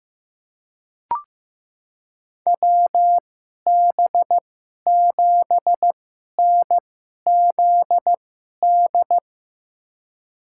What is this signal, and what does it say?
Morse code, heard as a keyed tone, about 15 wpm, spaced as for 11 wpm, 700 Hz. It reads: WB7NZD
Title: USA Call Signs - 6 Characters - 2x Character Spacing - 15wpm